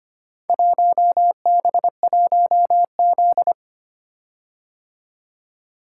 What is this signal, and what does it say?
Morse code, heard as a keyed tone, about 25 wpm, 700 Hz. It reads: 1B1Z
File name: practice / 25wpm / Four Letter-Number Combinations - 25wpm